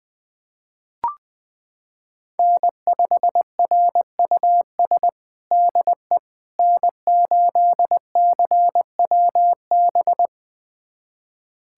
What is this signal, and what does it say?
Morse code, heard as a keyed tone, about 20 wpm, 700 Hz. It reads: N5RUS DE N8CWB
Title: USA Callsign DE Callsign - 5 Characters - 20wpm